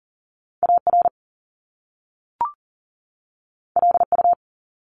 Morse code, heard as a keyed tone, about 42 wpm, 700 Hz. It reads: AR LU
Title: Top Two Letter Combinations in Words 40wpm